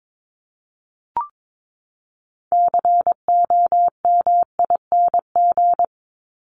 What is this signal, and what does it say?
Morse code, heard as a keyed tone, about 22 wpm, 700 Hz. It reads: COMING
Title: ING Suffix - Top 100 Words with Base Words 22wpm